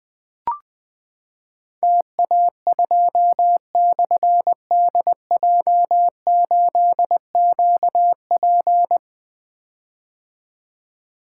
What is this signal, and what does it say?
Morse code, heard as a keyed tone, about 20 wpm, 700 Hz. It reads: TA2/DJ8QP
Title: Difficult Call Signs 20wpm